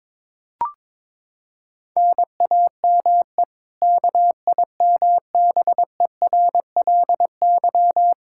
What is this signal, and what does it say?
Morse code, heard as a keyed tone, about 22 wpm, 700 Hz. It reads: NAME KIMBERLY
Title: Practice QSO - Exchange 1 - Beginning - 22wpm